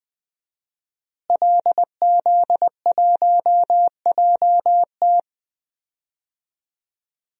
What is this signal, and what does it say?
Morse code, heard as a keyed tone, about 20 wpm, 700 Hz. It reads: LZ1JT
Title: DXCC-100 Call Signs - 5 Characters - Repeated 3x - 20wpm